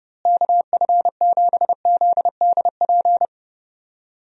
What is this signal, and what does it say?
Morse code, heard as a keyed tone, about 30 wpm, 700 Hz. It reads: KF7ZDP
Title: USA Call Signs - 6 Characters - Repeated 3x - 30wpm